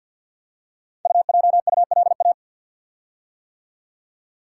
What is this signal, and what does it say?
Morse code, heard as a keyed tone, about 50 wpm, 700 Hz. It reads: UJULA
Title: Code Groups - 5 Letters - 50wpm